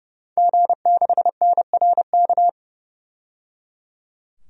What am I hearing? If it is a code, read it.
Morse code, heard as a keyed tone, about 30 wpm, 700 Hz. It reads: G6NRK
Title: DXCC-100 Call Signs - 5 Characters - 30wpm